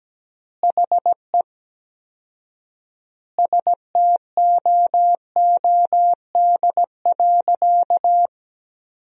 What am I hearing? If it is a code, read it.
Morse code, heard as a keyed tone, about 17 wpm, 700 Hz. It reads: HE STOOD.
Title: 4x Word-Spacing - Sentences from Top 500 Words - 17wpm